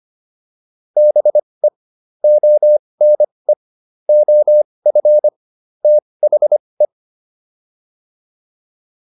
Morse code, heard as a keyed tone, about 25 wpm, spaced as for 20 wpm, 600 Hz. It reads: BE ONE OF THE
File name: WD304 20